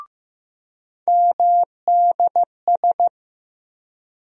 Morse code, heard as a keyed tone, about 15 wpm, 700 Hz. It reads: MDS
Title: USA Call Sign Postfixes - 3 Letters 15wpm